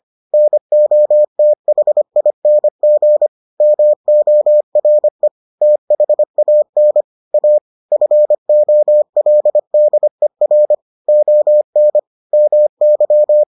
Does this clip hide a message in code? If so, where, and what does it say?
Yes: Morse code from 0.3 to 13.4 s, heard as a keyed tone, about 25 wpm, 600 Hz. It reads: NOTHING MORE THAN A FOLDER ON MY